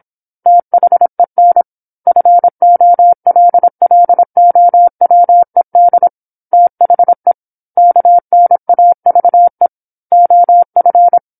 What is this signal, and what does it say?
Morse code, heard as a keyed tone, about 27 wpm, 700 Hz. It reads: THEN FOLLOWED THE KNAVE OF
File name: 08-Chapter-08 26 WPM